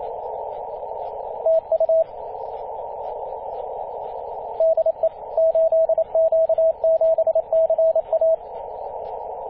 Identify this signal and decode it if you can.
Morse code, heard as a keyed tone, about 27 wpm, 650 Hz, with radio noise. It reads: TU DE 8Q7CA
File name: 8q7ca-21-013mhz-cw